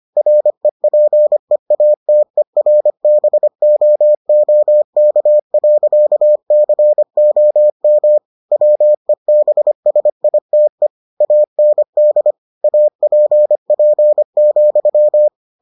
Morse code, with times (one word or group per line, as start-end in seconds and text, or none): 0.2-8.2 REPEATERBOOK.COM
8.5-10.9 WEBSITE
11.2-12.3 AND
12.6-15.3 APP,